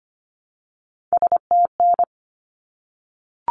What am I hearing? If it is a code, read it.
Morse code, heard as a keyed tone, about 25 wpm, 700 Hz. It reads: STN
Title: Abbreviations 25wpm